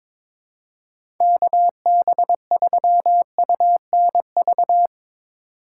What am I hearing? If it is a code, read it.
Morse code, heard as a keyed tone, about 22 wpm, 700 Hz. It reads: KB3UNV
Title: USA Call Signs - 6 Characters - 3x Character Spacing - 22wpm